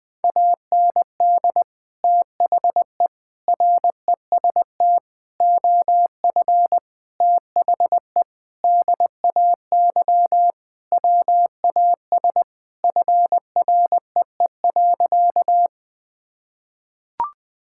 Morse code, heard as a keyed tone, about 21 wpm, 700 Hz. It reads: AND THE REST OF THE DAY WAS FREE.
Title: Sentences with 4-Word Phrases - Spoken Beginning - 20wpm